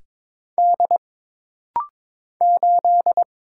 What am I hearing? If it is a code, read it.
Morse code, heard as a keyed tone, about 22 wpm, 700 Hz. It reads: D 8
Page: This rapid-fire timing is geared to push your ICR (Instant Character Recognition) skills.